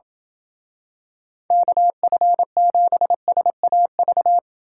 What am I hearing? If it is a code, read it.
Morse code, heard as a keyed tone, about 27 wpm, 700 Hz. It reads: KF7SAV